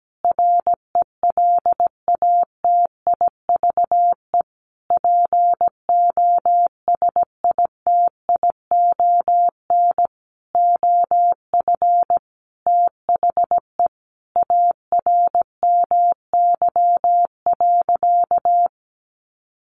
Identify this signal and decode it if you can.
Morse code, heard as a keyed tone, about 17 wpm, 700 Hz. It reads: RELATIVE POSITION OF THE ARMY.
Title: Sentences from Top 1400 Words 17wpm